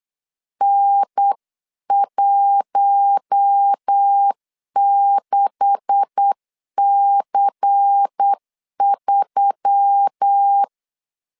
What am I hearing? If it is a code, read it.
Morse code, heard as a keyed tone, about 8 wpm, 800 Hz. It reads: N16C3
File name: LEZIONI / 9 286 8wpm 3425